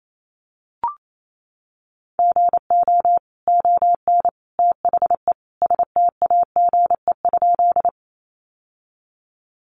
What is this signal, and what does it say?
Morse code, heard as a keyed tone, about 28 wpm, 700 Hz. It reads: GO ON THE STAGE?
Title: Sentences from Top 1200 Words 28wpm